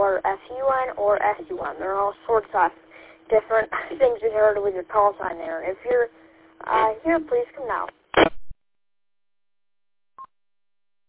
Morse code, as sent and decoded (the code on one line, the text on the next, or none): none